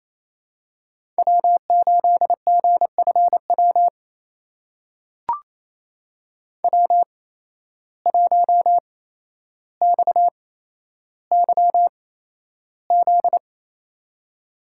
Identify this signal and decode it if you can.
Morse code, heard as a keyed tone, about 28 wpm, spaced as for 12 wpm, 700 Hz. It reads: W8GFW W1XYZ
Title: USA Call Signs - 5 Characters - 8x Character Spacing - 28wpm